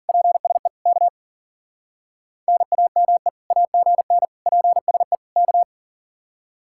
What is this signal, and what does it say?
Morse code, heard as a keyed tone, about 50 wpm, 700 Hz. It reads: PSE K NAME AGN PSE K
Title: Speed-Racing Common QSO Phrases Encore - 40wpm